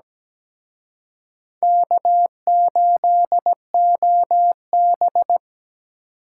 Morse code, heard as a keyed tone, about 20 wpm, 700 Hz. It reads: K8OB
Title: Speed Racing - USA 1x2 Call Signs 17wpm